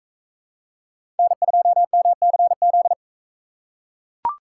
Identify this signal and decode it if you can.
Morse code, heard as a keyed tone, about 40 wpm, 700 Hz. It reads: N2MCZ